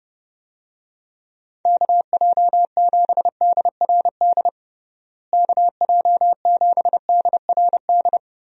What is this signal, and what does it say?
Morse code, heard as a keyed tone, about 31 wpm, 700 Hz. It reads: KJ7DRD KJ7DRD
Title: USA Call Signs - 6 Characters - Repeated 2x - 30wpm